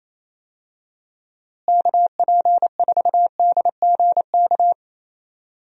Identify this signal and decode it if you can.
Morse code, heard as a keyed tone, about 28 wpm, 700 Hz. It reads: KP4DGK
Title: USA Call Signs - 6 Characters - Repeated 2x - 28wpm